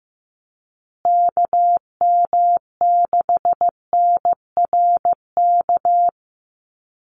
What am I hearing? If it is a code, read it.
Morse code, heard as a keyed tone, about 15 wpm, 700 Hz. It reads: KM6NRK
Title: USA Call Signs Repeated 2x 15wpm